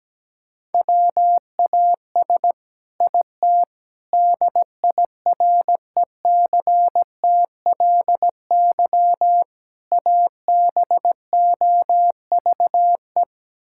Morse code, heard as a keyed tone, about 17 wpm, 700 Hz. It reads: WAS IT DIRECTLY ABOVE
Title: Sentences from Top 1000 Words - Review Part 5 - 17wpm